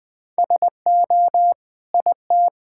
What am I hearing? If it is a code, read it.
Morse code, heard as a keyed tone, about 21 wpm, 700 Hz. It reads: SO IT
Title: Sentences with 8-Word Phrases - Spoken Beginning - 20wpm